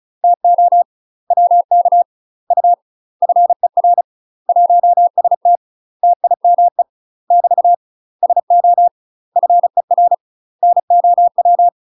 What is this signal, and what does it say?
Morse code, heard as a keyed tone, about 36 wpm, 700 Hz. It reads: TO WK U FER 1ST TIME <BT> SO FER NOW
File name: UK MASTER QSOs TRAINING / 35WPM / UK MASTER QSOs 0018